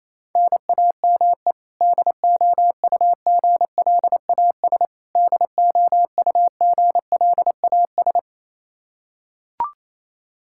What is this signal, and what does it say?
Morse code, heard as a keyed tone, about 28 wpm, 700 Hz. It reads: NAME DOUGLAS DOUGLAS
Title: Practice QSO - Exchange 1 - Beginning - 28wpm